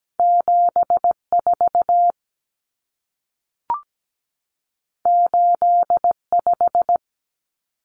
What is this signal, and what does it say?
Morse code, heard as a keyed tone, about 17 wpm, 700 Hz. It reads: 74 85
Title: Numbers - 2-Digit 17wpm